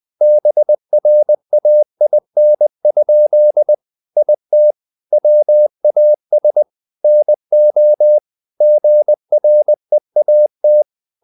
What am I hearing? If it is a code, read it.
Morse code, heard as a keyed tone, about 20 wpm, 600 Hz. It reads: BRAIN? IT WAS NO GREAT
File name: the old man and the sea part 2 20wpm